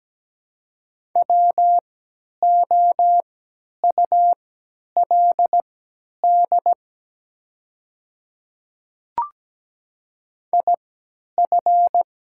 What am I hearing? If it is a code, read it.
Morse code, heard as a keyed tone, about 17 wpm, spaced as for 10 wpm, 700 Hz. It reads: WOULD IF
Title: 3x Character Spacing - Top 100 Words - 17wpm